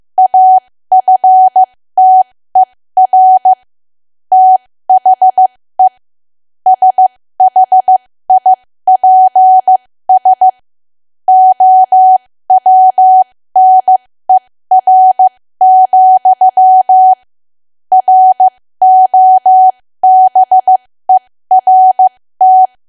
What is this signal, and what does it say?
Morse code, heard as a keyed tone, about 15 wpm, spaced as for 13 wpm, 750 Hz. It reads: AFTER THE SHIPS OWNER, ROBERT
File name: cwa / W1AW-Level2 / 13WPM